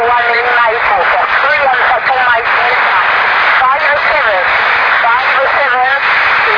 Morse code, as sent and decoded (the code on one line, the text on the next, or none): none